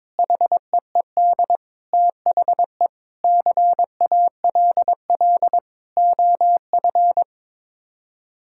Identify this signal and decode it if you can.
Morse code, heard as a keyed tone, about 22 wpm, 700 Hz. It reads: HEED THE CALL OF
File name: practice / 22wpm / Sets of 8 Words - 22wpm